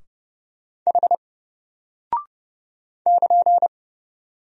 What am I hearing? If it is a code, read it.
Morse code, heard as a keyed tone, about 29 wpm, 700 Hz. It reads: H <KN>
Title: Single Character (Letter:Number:Punctuation:Prosign) 30wpm